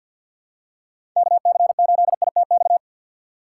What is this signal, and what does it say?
Morse code, heard as a keyed tone, about 47 wpm, 700 Hz. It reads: KC8ITX
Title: USA Call Signs - 6x Character Spacing - 50wpm